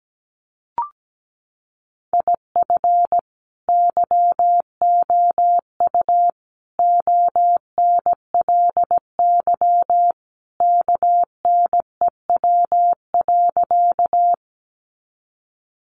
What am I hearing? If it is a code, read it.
Morse code, heard as a keyed tone, about 17 wpm, 700 Hz. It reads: IF YOU ONLY KNEW.
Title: Sentences from Top 400 Words 17wpm